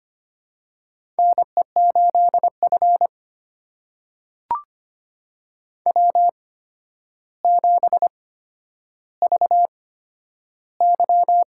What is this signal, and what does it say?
Morse code, heard as a keyed tone, about 25 wpm, spaced as for 11 wpm, 700 Hz. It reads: NE8F W7VY
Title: USA Call Signs - 4 Characters - 8x Character Spacing - 25wpm